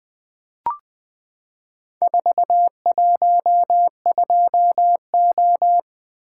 Morse code, heard as a keyed tone, about 20 wpm, 700 Hz. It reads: 412O
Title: Four Letter-Number Combinations - 20wpm